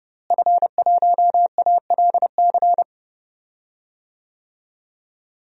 Morse code, heard as a keyed tone, about 30 wpm, 700 Hz. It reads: F1ALC
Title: DXCC-100 Call Signs - 5 Characters - Repeated 3x - 30wpm